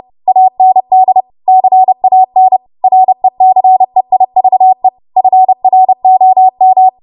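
Morse code, heard as a keyed tone, about 30 wpm, 750 Hz. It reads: AND CAN RECEIVE FROM
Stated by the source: Morse practice transmission at 30 WPM.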